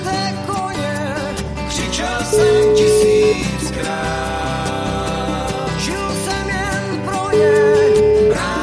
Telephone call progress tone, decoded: ringback tone

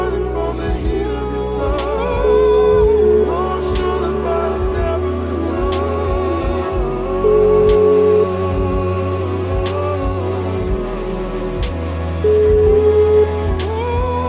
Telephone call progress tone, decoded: ringback tone

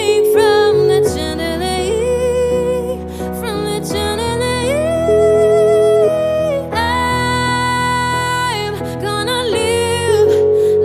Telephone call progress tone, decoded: ringback tone

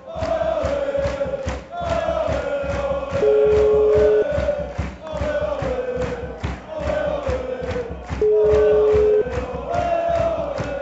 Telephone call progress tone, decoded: ringback tone

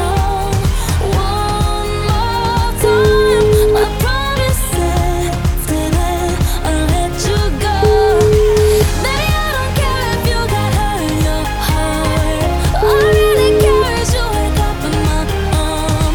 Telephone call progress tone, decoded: ringback tone

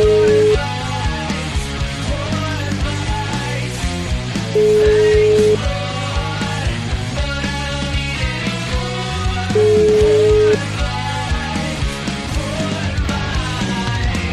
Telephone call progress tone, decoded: ringback tone

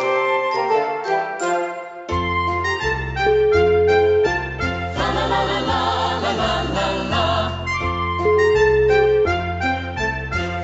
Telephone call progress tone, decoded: ringback tone

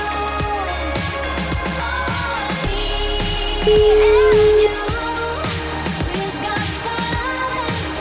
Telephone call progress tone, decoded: ringback tone

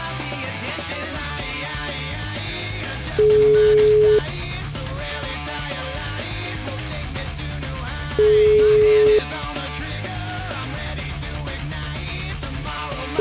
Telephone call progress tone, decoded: ringback tone

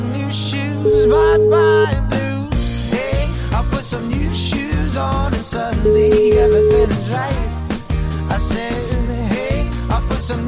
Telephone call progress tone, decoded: ringback tone

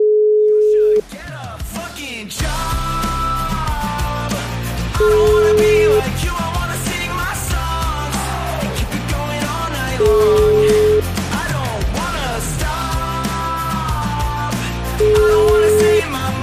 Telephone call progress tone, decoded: ringback tone

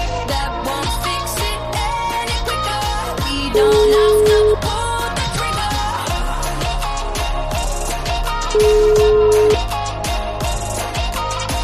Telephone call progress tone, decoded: ringback tone

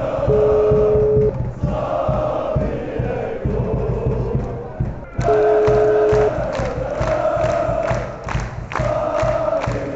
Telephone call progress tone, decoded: ringback tone

